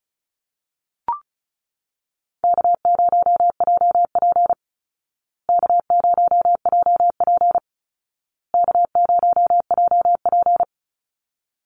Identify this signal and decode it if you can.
Morse code, heard as a keyed tone, about 34 wpm, 700 Hz. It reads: K0JP K0JP K0JP
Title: Call Signs Repeated 2x - 35wpm